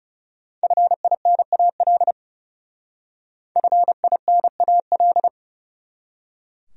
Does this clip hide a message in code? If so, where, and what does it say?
Yes: Morse code from 0.6 to 5.3 s, heard as a keyed tone, about 38 wpm, 700 Hz. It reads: FINAL FINAL